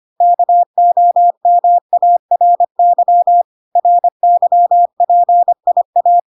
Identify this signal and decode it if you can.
Morse code, heard as a keyed tone, about 25 wpm, 700 Hz. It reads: KOMARY RYPIA